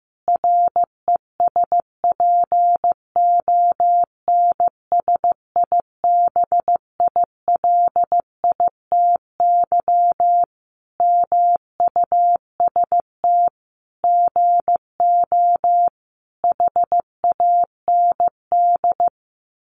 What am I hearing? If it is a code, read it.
Morse code, heard as a keyed tone, about 15 wpm, 700 Hz. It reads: RESPONSIBILITY MUST GO HAND
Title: Sentences from Top 1200 Words 15wpm